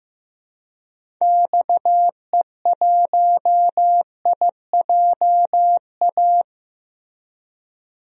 Morse code, heard as a keyed tone, about 15 wpm, 700 Hz. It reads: XE1IJA
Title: DXCC-100 Call Signs - 6 Characters - Repeated 2x - 15wpm